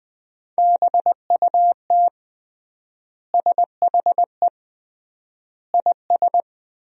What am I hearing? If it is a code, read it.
Morse code, heard as a keyed tone, about 20 wpm, 700 Hz. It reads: BUT SHE IS
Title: 3x Word-Spacing - Sentences from Top 400 Words - 20wpm